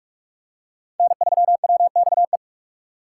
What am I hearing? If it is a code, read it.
Morse code, heard as a keyed tone, about 41 wpm, 700 Hz. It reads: N3WXE
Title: USA Call Signs - 5 Characters - 3x Character Spacing - 45wpm